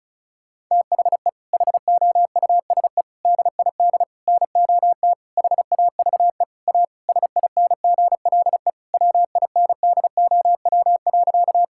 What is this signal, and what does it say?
Morse code, heard as a keyed tone, about 36 wpm, 700 Hz. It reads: THE HOUSE DID NOT HAVE A SINGLE WINDOW.
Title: Sentences from Top 1600 Words 35wpm